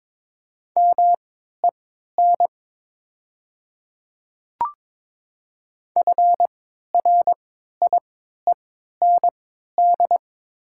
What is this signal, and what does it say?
Morse code, heard as a keyed tone, about 22 wpm, spaced as for 12 wpm, 700 Hz. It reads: MEN FRIEND
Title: 3x Character Spacing - Top 500 Words - 22wpm